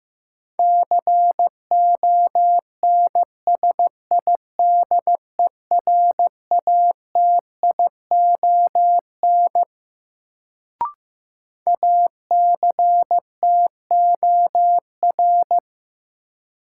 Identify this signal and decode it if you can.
Morse code, heard as a keyed tone, about 15 wpm, 700 Hz. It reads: CONSIDERATION ACTOR
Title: Sentences from Top 1500 Words - New Word Drill Alt - 15wpm